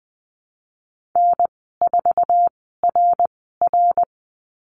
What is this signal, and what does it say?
Morse code, heard as a keyed tone, about 20 wpm, spaced as for 15 wpm, 700 Hz. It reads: N4RR
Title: 055 Lesson 10g - Letter r - Call Signs 20wpm